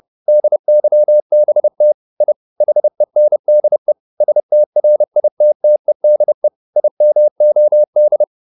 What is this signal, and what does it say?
Morse code, heard as a keyed tone, about 30 wpm, 600 Hz. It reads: DYBT I HENDE STRITTEDE IMOD
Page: Ferie 30 wpm